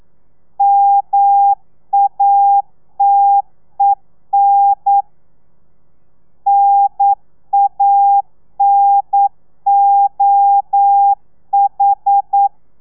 Morse code, heard as a keyed tone, about 9 wpm, 800 Hz. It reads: MATEN NANOH
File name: telegrafia / i2tfj / Lez2 40